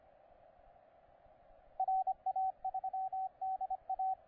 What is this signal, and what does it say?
Morse code, heard as a keyed tone, about 26 wpm, 750 Hz, with radio noise. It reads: RA3DA
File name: VP2V/SP9WZS 10CW